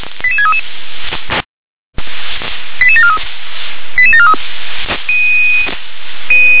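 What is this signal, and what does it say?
Five-tone selective call (ZVEI), heard as quick runs of five tones, tones of about 70 ms. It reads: 88633 88743 88743
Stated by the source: ZVEI five-tone call received on SDR.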